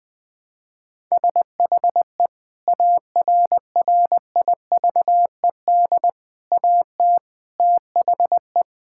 Morse code, heard as a keyed tone, about 21 wpm, 700 Hz. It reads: SHE ARRIVED AT THE
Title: Sentences with 5-Word Phrases - 20wpm